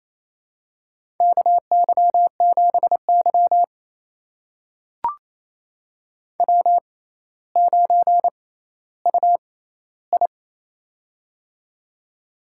Morse code, heard as a keyed tone, about 28 wpm, spaced as for 14 wpm, 700 Hz. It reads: KY7Y W9UI
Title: USA Call Signs - 4 Characters - 6x Character Spacing - 28wpm